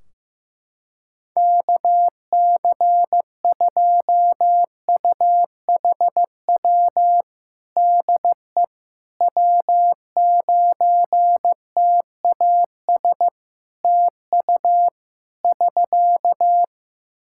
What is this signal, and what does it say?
Morse code, heard as a keyed tone, about 15 wpm, 700 Hz. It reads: KC2UHW DE W9TAS TU <SK>
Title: Practice QSO - Exchange 4 - Ending - 15wpm